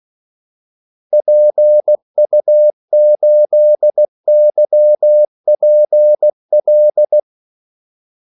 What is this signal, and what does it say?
Morse code, heard as a keyed tone, about 16 wpm, 600 Hz. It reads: PU8YPL
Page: Kaldesignaler 16 wpm